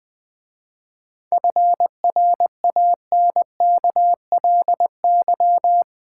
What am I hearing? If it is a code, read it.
Morse code, heard as a keyed tone, about 20 wpm, 700 Hz. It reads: FRANKLY
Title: 7 Letter Words - First 3 Letters Given - 20wpm